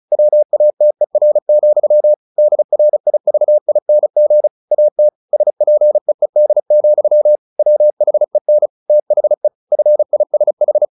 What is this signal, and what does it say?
Morse code, heard as a keyed tone, about 34 wpm, 600 Hz. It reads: WATER, DRIVING AT SPEED, WHEN THE FISH
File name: the old man and the sea / the old man and the sea 1 35wpm